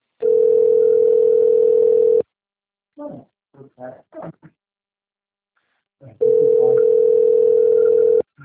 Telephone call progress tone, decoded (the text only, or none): ringback tone